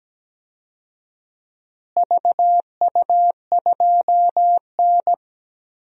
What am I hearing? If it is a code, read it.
Morse code, heard as a keyed tone, about 17 wpm, 700 Hz. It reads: VU2N
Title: DXCC-100 Call Signs - 4 Characters - Repeated 3x - 17wpm